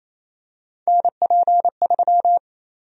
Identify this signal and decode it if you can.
Morse code, heard as a keyed tone, about 27 wpm, 700 Hz. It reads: NP3
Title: USA Call Sign Prefixes - Repeated 2x - 28wpm